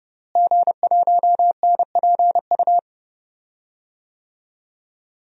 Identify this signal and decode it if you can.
Morse code, heard as a keyed tone, about 30 wpm, 700 Hz. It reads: G1NPU
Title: DXCC-100 Call Signs - 5 Characters - Repeated 2x - 30wpm